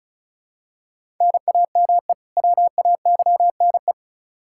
Morse code, heard as a keyed tone, about 36 wpm, 700 Hz. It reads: NAME WAYNE
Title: Practice QSO - Exchange 1 - Beginning - 35wpm